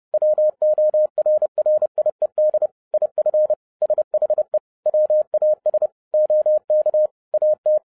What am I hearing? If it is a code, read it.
Morse code, heard as a keyed tone, about 30 wpm, 610 Hz. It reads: WORRIED IF SHE WAS OK AT